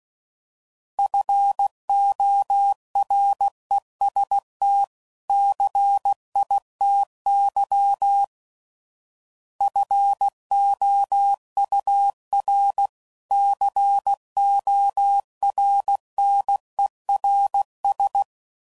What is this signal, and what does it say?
Morse code, heard as a keyed tone, about 16 wpm, 800 Hz. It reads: FOREST CITY FOUR CORNERS
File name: CW Code Groups @15WPM